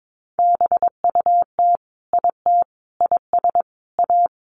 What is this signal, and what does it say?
Morse code, heard as a keyed tone, about 22 wpm, 700 Hz. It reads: BUT IT IS A
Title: Sentences from Top 1500 Words 22wpm